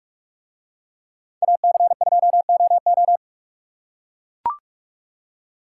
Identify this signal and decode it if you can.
Morse code, heard as a keyed tone, about 41 wpm, 700 Hz. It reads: AC2OO